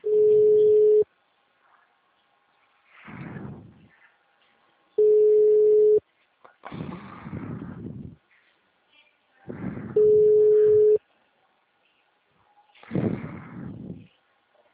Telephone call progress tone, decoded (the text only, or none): ringback tone